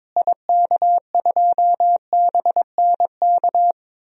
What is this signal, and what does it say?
Morse code, heard as a keyed tone, about 22 wpm, 700 Hz. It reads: IK2BNK
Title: DXCC-100 Call Signs - 6 Characters - Repeated 2x - 22wpm